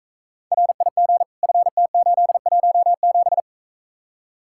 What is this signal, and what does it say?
Morse code, heard as a keyed tone, about 43 wpm, 700 Hz. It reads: RIG FT817